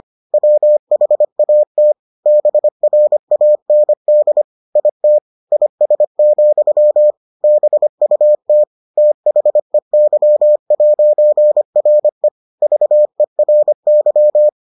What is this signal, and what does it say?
Morse code, heard as a keyed tone, about 25 wpm, 600 Hz. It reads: WHAT BRAND IT IS, BUT THEY'RE VERY